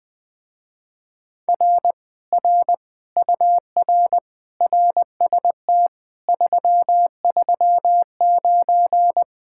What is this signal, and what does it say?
Morse code, heard as a keyed tone, about 21 wpm, 700 Hz. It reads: R R UR RST 339
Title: Speed-Racing Common QSO Phrases - 20wpm